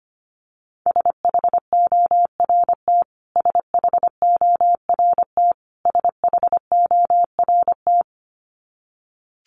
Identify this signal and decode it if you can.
Morse code, heard as a keyed tone, about 25 wpm, 700 Hz. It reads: SHORT SHORT SHORT
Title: Top 500 Words Repeated 25wpm